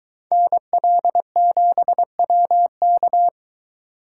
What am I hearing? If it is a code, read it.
Morse code, heard as a keyed tone, about 23 wpm, 700 Hz. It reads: NL7WK